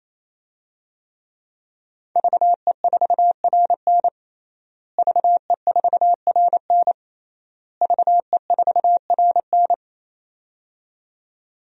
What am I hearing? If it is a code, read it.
Morse code, heard as a keyed tone, about 28 wpm, 700 Hz. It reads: VE4RN VE4RN VE4RN
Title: DXCC-100 Call Signs - 5 Characters - Repeated 3x - 28wpm